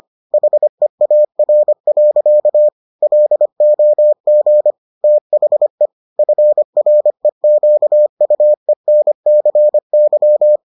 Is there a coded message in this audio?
Yes: Morse code from 0.3 to 10.6 s, heard as a keyed tone, about 25 wpm, 600 Hz. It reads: HEAR. LOG THE FREQUENCY